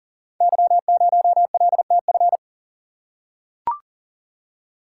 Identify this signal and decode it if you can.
Morse code, heard as a keyed tone, about 39 wpm, 700 Hz. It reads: Y0LTF
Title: Code Groups - 5 Letter:Numbers - 40wpm